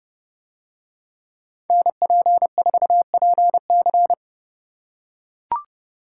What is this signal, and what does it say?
Morse code, heard as a keyed tone, about 30 wpm, 700 Hz. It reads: NP4PC